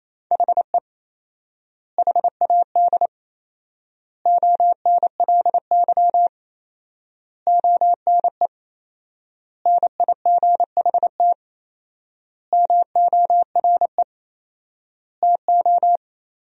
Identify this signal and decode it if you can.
Morse code, heard as a keyed tone, about 28 wpm, 700 Hz. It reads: HE HAD ONLY ONE NIGHT MORE TO
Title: 4x Word-Spacing - Sentences from Top 300 Words - 28wpm